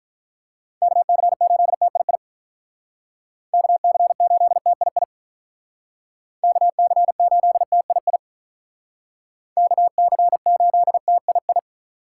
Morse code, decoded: KC8TII KC8TII KC8TII KC8TII

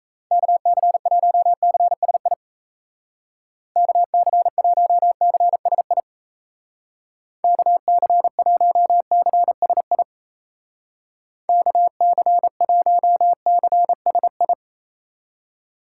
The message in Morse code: KC1CSI KC1CSI KC1CSI KC1CSI

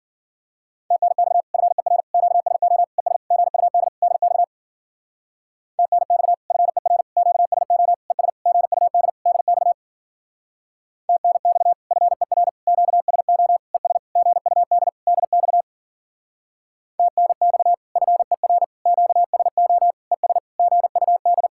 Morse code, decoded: TNX FER QSO ES GUD DX TNX FER QSO ES GUD DX TNX FER QSO ES GUD DX TNX FER QSO ES GUD

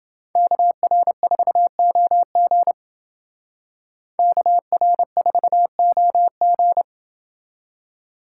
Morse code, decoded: KR4OG KR4OG

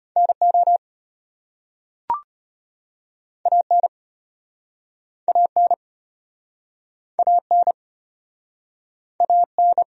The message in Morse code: NO AN AN AN AN

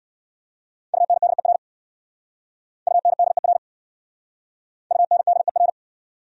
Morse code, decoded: UNDER UNDER UNDER

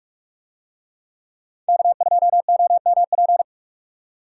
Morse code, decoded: K2OMP

